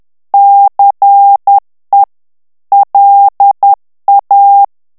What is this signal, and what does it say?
Morse code, heard as a keyed tone, about 10 wpm, 800 Hz. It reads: CE LA
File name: morse mp3 / ITALIEN texte clair 1 vitesse 10 mots minute